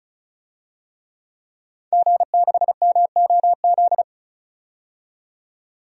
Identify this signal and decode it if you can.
Morse code, heard as a keyed tone, about 36 wpm, 700 Hz. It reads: G6MOZ
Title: DXCC-100 Call Signs - 5 Characters - 35wpm